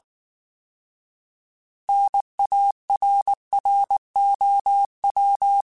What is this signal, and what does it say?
Morse code, heard as a keyed tone, about 19 wpm, 800 Hz. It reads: NARROW
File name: Alphabetical / Alphabetical mp3s / Ns @18